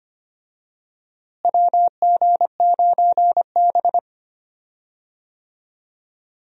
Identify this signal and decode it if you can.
Morse code, heard as a keyed tone, about 25 wpm, 700 Hz. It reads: WG9B
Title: USA Call Signs - 4 Characters - Repeated 2x - 25wpm